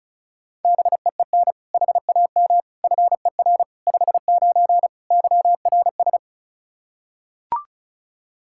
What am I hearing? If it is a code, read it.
Morse code, heard as a keyed tone, about 36 wpm, 700 Hz. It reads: BEEN HAM FER 59 YRS
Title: Practice QSO - Exchange 3 - More Info - 35wpm